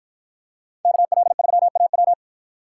Morse code, heard as a keyed tone, about 50 wpm, 700 Hz. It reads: KL3AW